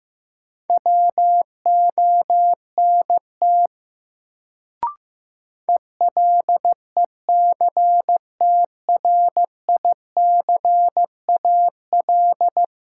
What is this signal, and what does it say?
Morse code, heard as a keyed tone, about 15 wpm, 700 Hz. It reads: WONT ELECTRICAL
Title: Sentences from Top 1000 Words - Review Part 6 - New Word Drill - 15wpm